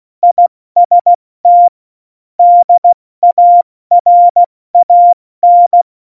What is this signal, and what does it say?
Morse code, heard as a keyed tone, about 16 wpm, 700 Hz. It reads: IST DARAN
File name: Morsekurs / Witze / Witz 15. September 2020, 70 F80